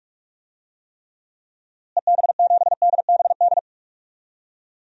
Morse code, heard as a keyed tone, about 46 wpm, 700 Hz. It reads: EB7DBD
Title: DXCC-100 Call Signs - 6 Characters - Repeated 3x - 45wpm